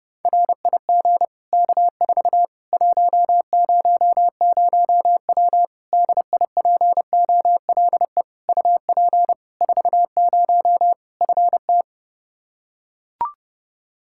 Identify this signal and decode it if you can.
Morse code, heard as a keyed tone, about 30 wpm, 700 Hz. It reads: RIG K4 100W DIPOLE UP 40 FT
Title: Practice QSO - Exchange 2 - Info - 30wpm